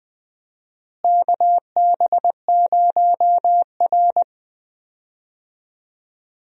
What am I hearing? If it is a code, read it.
Morse code, heard as a keyed tone, about 20 wpm, 700 Hz. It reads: KB0R
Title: USA Call Signs - 4 Characters - Repeated 3x - 20wpm